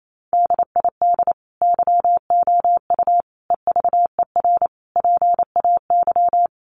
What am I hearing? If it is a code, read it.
Morse code, heard as a keyed tone, about 28 wpm, 700 Hz. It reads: DID YOU EVER PAY
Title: Sentences from Top 700 Words 28wpm